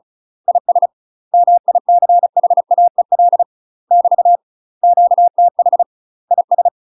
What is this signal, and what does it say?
Morse code, heard as a keyed tone, about 36 wpm, 700 Hz. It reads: IS MICHAEL <BT> QTH IS
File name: UK MASTER QSOs TRAINING / 35WPM / UK MASTER QSOs 0041